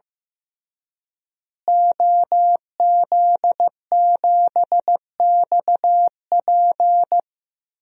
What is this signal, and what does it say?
Morse code, heard as a keyed tone, about 15 wpm, 700 Hz. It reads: OZ7XP
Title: DXCC-100 Call Signs - 5 Characters - Repeated 2x - 15wpm